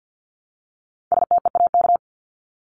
Morse code, heard as a keyed tone, about 50 wpm, 700 Hz. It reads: STEAK
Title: Top 1251-1500 Words 50wpm